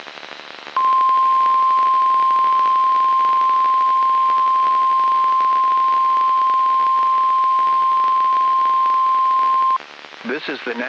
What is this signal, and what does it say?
Emergency Alert System (SAME) tones (the 1050 Hz attention tone), with speech. It is heard from 0.7 to 9.8 s.